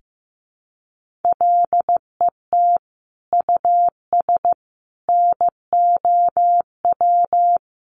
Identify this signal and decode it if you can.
Morse code, heard as a keyed tone, about 15 wpm, 700 Hz. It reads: LET US NOW
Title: Sentences from Top 1500 Words 15wpm